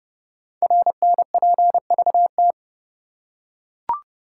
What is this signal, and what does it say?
Morse code, heard as a keyed tone, about 30 wpm, 690 Hz. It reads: RNPVT
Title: Code Groups - 5 Letters - 30wpm